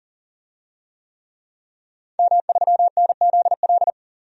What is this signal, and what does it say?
Morse code, heard as a keyed tone, about 40 wpm, 700 Hz. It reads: M3NZL
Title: DXCC-100 Call Signs - 5 Characters - 40wpm